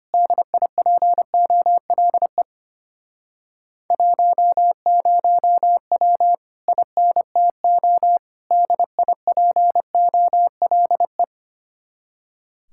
Morse code, heard as a keyed tone, about 32 wpm, 700 Hz. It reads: DIPOLE 10W INTO DIPOLE